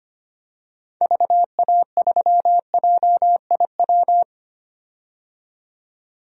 Morse code, heard as a keyed tone, about 25 wpm, 700 Hz. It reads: VA3JIW
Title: DXCC-100 Call Signs - 6 Characters - 25wpm